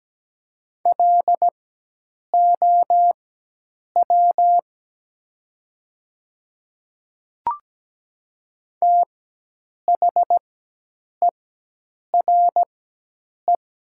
Morse code, heard as a keyed tone, about 17 wpm, spaced as for 8 wpm, 700 Hz. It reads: LOW THERE
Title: 4x Character Spacing - Top 300 Words - 17wpm